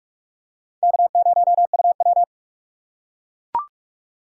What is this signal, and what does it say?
Morse code, heard as a keyed tone, about 38 wpm, 700 Hz. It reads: K0UW